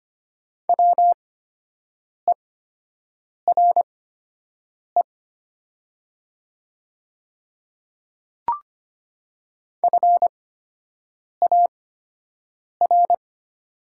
Morse code, heard as a keyed tone, about 25 wpm, spaced as for 7 wpm, 700 Hz. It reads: WERE FAR
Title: 8x Character Spacing - Top 300 Words - 25wpm